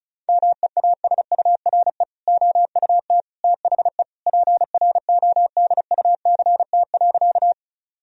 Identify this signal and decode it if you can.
Morse code, heard as a keyed tone, about 36 wpm, 700 Hz. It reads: MEASURE OUT THE PRODUCT.
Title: Sentences from Top 1000 Words - Review Part 2 - 35wpm